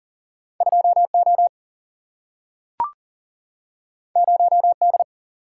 Morse code, heard as a keyed tone, about 40 wpm, 700 Hz. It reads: 2O 0D
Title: Single-Letter Single-Number 40wpm